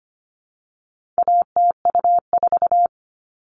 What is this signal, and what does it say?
Morse code, heard as a keyed tone, about 25 wpm, 700 Hz. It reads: ATU4
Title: All Cut Numbers - 4-Digit 25wpm